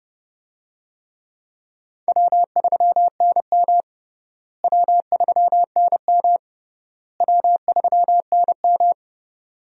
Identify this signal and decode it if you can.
Morse code, heard as a keyed tone, about 31 wpm, 700 Hz. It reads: W3NM W3NM W3NM
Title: USA Call Signs - 4 Characters - Repeated 3x - 30wpm